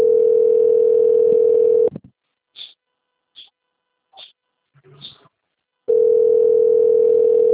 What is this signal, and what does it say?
Ringback tone (North America), 440 + 480 Hz.
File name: out- - - - - .32813